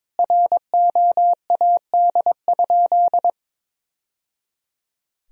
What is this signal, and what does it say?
Morse code, heard as a keyed tone, about 22 wpm, 700 Hz. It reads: ROAD?
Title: Sentences from Top 1000 Words - Review Part 2 - 22wpm